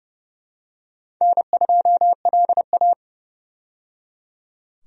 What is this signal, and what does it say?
Morse code, heard as a keyed tone, about 30 wpm, 700 Hz. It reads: N2LA
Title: USA Call Signs - 4 Characters - 30wpm